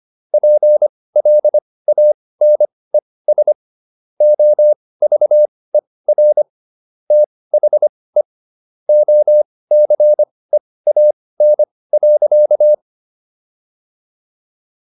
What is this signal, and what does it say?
Morse code, heard as a keyed tone, about 25 wpm, spaced as for 18 wpm, 600 Hz. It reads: PLANES OVER THE OCEAN.
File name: SS108 18